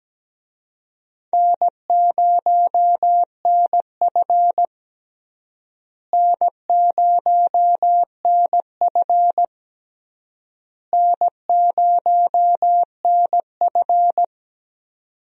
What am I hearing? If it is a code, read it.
Morse code, heard as a keyed tone, about 17 wpm, 700 Hz. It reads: N0NF N0NF N0NF
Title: USA Call Signs - 4 Characters - Repeated 3x - 17wpm